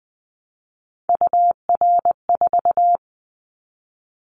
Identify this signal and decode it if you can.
Morse code, heard as a keyed tone, about 20 wpm, 700 Hz. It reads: UR4
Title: Callsign Prefixes for Top 25 DXCC - Repeated 3x - 20wpm